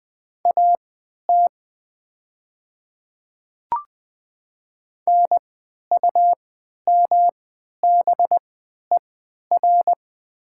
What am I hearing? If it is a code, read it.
Morse code, heard as a keyed tone, about 20 wpm, spaced as for 11 wpm, 700 Hz. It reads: AT NUMBER
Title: 3x Character Spacing - Top 100 Words - 20wpm